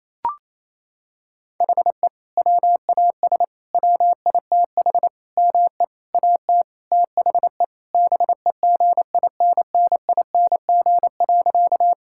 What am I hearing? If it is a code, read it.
Morse code, heard as a keyed tone, about 27 wpm, 700 Hz. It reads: HE WAS WITH ME AT THE BEGINNING.
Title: Sentences from Top 1000 Words - Review Part 5 - 28wpm